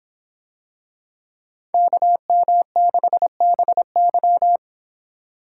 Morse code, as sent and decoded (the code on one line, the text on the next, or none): -.- -- -.... -... -.--
KM6BY